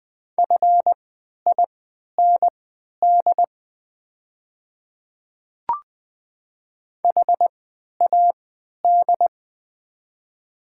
Morse code, heard as a keyed tone, about 20 wpm, spaced as for 11 wpm, 700 Hz. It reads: FIND HAD
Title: 3x Character Spacing - Top 300 Words - 20wpm